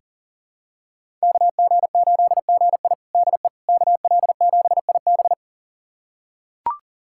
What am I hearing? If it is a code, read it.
Morse code, heard as a keyed tone, about 37 wpm, 700 Hz. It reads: KG8GI DE KL7IB
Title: USA Callsign DE Callsign 40wpm